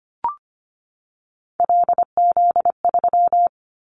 Morse code, heard as a keyed tone, about 25 wpm, 700 Hz. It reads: LZ3
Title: Callsign Prefixes for Top 100 DXCC - 25wpm